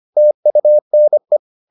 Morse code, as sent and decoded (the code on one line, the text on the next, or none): - ..- -. .
TUNE